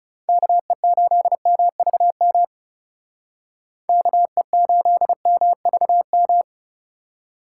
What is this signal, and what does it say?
Morse code, heard as a keyed tone, about 37 wpm, 700 Hz. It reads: KE8MVM KE8MVM
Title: Speed Racing - USA 2x3 Call Signs 30wpm